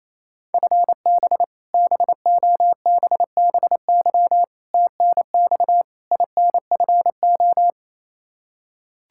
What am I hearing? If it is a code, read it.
Morse code, heard as a keyed tone, about 28 wpm, 700 Hz. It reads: FB BOBBY TNX INFO
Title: Practice QSO - Exchange 3 - More Info - 28wpm